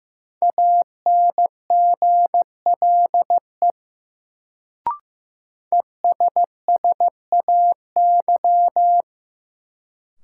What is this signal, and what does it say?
Morse code, heard as a keyed tone, about 15 wpm, 700 Hz. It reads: ANGLE ESSAY
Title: Sentences from Top 1500 Words - New Word Drill - 15wpm